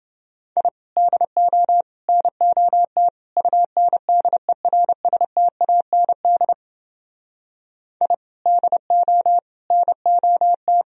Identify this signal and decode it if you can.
Morse code, heard as a keyed tone, about 29 wpm, 700 Hz. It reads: I DO NOT UNDERSTAND I DO NOT